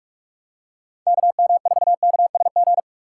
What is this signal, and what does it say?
Morse code, heard as a keyed tone, about 44 wpm, 700 Hz. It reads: KM4KSG
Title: USA Call Signs - 6 Characters - 6x Character Spacing - 45wpm